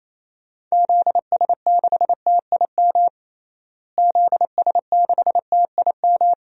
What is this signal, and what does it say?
Morse code, heard as a keyed tone, about 27 wpm, 700 Hz. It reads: ZS6TIM ZS6TIM
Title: DXCC-100 Call Signs - 6 Characters - Repeated 2x - 28wpm